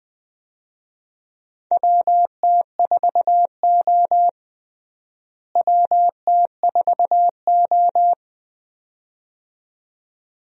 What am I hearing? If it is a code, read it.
Morse code, heard as a keyed tone, about 20 wpm, 700 Hz. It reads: WT4O WT4O
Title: USA Call Signs - 4 Characters - Repeated 2x - 20wpm